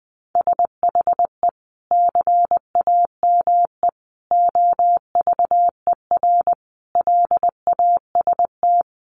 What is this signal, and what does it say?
Morse code, heard as a keyed tone, about 20 wpm, 700 Hz. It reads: SHE CAME OVER LAST
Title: Sentences from Top 300 Words 20wpm